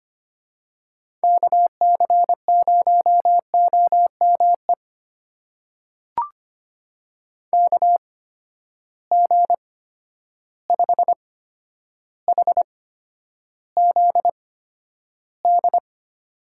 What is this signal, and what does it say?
Morse code, heard as a keyed tone, about 25 wpm, spaced as for 11 wpm, 700 Hz. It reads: KC0OME KG5HZD
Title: USA Call Signs - 6 Characters - 8x Character Spacing - 25wpm